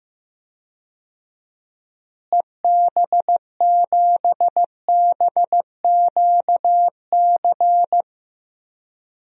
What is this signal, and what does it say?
Morse code, heard as a keyed tone, about 15 wpm, 700 Hz. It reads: EB7BQC
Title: DXCC-100 Call Signs - 6 Characters - Repeated 2x - 15wpm